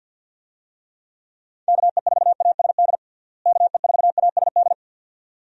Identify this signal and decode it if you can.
Morse code, heard as a keyed tone, about 52 wpm, 700 Hz. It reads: KE4ASD KE4ASD
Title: USA Call Signs - 6 Characters - Repeated 2x - 50wpm